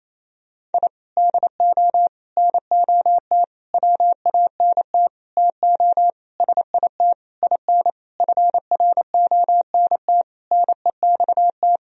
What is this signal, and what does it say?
Morse code, heard as a keyed tone, about 28 wpm, 700 Hz. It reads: I DO NOT WANT TO SIT IN FRONT NEXT
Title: Sentences from Top 1000 Words - Review Part 2 - 28wpm